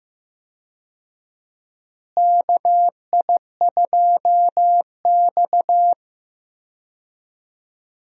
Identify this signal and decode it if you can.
Morse code, heard as a keyed tone, about 16 wpm, 700 Hz. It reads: KI2X
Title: USA Call Signs - 4 Characters - Repeated 3x - 15wpm